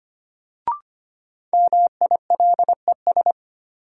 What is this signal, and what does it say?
Morse code, heard as a keyed tone, about 25 wpm, 700 Hz. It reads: MILES
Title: Sentences from Top 1000 Words - Review Part 3 - New Word Drill - 25wpm